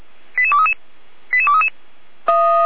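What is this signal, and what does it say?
Five-tone selective call (ZVEI), heard as quick runs of five tones, tones of about 70 ms. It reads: 88233 88233
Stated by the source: ZVEI five-tone call received on SDR.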